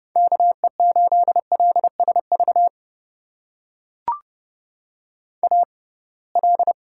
Morse code, heard as a keyed tone, about 30 wpm, spaced as for 23 wpm, 700 Hz. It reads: KE8LSV AL
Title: USA Call Signs - 6x Character Spacing - 30wpm